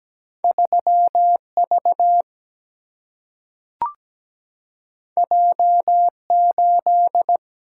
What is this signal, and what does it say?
Morse code, heard as a keyed tone, about 17 wpm, 700 Hz. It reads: 3 V J 8
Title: Single-Letter Single-Number 17wpm